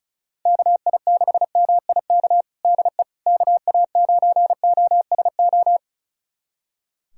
Morse code, heard as a keyed tone, about 36 wpm, 700 Hz. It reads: KI6MIK DE KA9OSO
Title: USA Callsign DE Callsign - 6 Characters - 35wpm